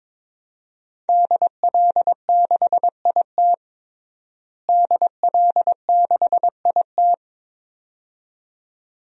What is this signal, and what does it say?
Morse code, heard as a keyed tone, about 22 wpm, 700 Hz. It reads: DL6IT DL6IT
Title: DXCC-100 Call Signs - 5 Characters - Repeated 2x - 22wpm